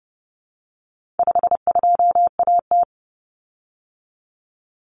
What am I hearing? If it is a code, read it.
Morse code, heard as a keyed tone, about 30 wpm, 700 Hz. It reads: 52AT
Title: Common Cut Numbers - 4-Digit 30wpm